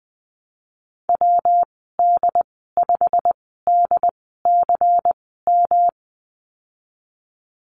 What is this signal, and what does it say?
Morse code, heard as a keyed tone, about 19 wpm, spaced as for 14 wpm, 700 Hz. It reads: WD5DCM
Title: 193 Lesson 31g - Letter v - Call Signs 20wpm